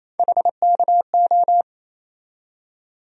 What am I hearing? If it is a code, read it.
Morse code, heard as a keyed tone, about 28 wpm, 700 Hz. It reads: HKO
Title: USA Call Sign Postfixes - 3 Letters 28wpm